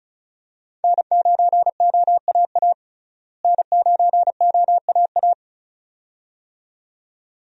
Morse code, heard as a keyed tone, about 34 wpm, 700 Hz. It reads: N9OAA N9OAA
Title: USA Call Signs - 5 Characters - Repeated 2x - 35wpm